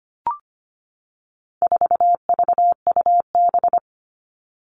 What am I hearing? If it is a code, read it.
Morse code, heard as a keyed tone, about 25 wpm, 700 Hz. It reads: 4VUB
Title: All Cut Numbers - 4-Digit 25wpm